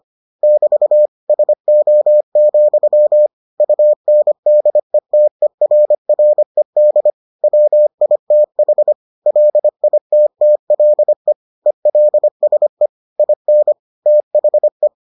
Morse code, heard as a keyed tone, about 25 wpm, 600 Hz. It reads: <BT> SO, UNDETERRED WITH LITTLE ELSE IN THE